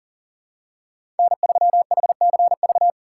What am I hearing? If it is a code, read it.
Morse code, heard as a keyed tone, about 40 wpm, 700 Hz. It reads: N3HCV
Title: USA Call Signs - 5 Characters - 6x Character Spacing - 40wpm